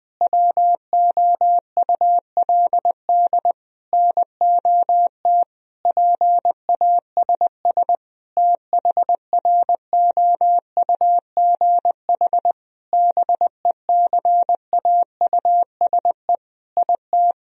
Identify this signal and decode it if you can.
Morse code, heard as a keyed tone, about 20 wpm, 700 Hz. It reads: WOULD NOT PASS THROUGH BECAUSE IT
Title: Sentences from Top 1000 Words - Review Part 1 - 20wpm